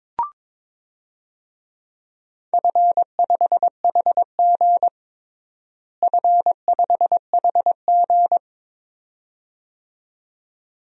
Morse code, heard as a keyed tone, about 22 wpm, 700 Hz. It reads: F5HG F5HG
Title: DXCC-100 Call Signs - 4 Characters - Repeated 2x - 22wpm